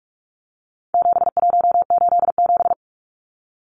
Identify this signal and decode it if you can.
Morse code, heard as a keyed tone, about 44 wpm, 700 Hz. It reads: 7187
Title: Numbers - 4-Digit 45wpm